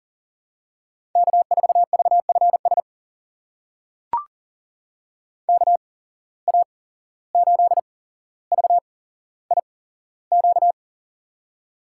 Morse code, heard as a keyed tone, about 40 wpm, spaced as for 16 wpm, 690 Hz. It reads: K4VFS KA8VIQ
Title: USA Call Signs - 8x Character Spacing - 40wpm